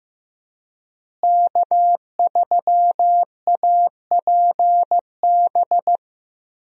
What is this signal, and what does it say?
Morse code, heard as a keyed tone, about 15 wpm, 700 Hz. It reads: K3APB
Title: USA Call Signs - 5 Characters - Repeated 2x - 15wpm